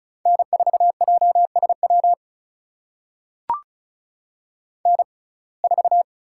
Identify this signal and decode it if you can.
Morse code, heard as a keyed tone, about 36 wpm, spaced as for 25 wpm, 700 Hz. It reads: N4JSW N4
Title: USA Call Signs - 5 Characters - 6x Character Spacing - 35wpm